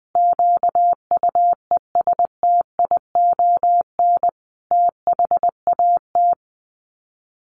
Morse code, heard as a keyed tone, about 20 wpm, 700 Hz. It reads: QUESTION THAT
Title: Sets of 5 Words Encore 20wpm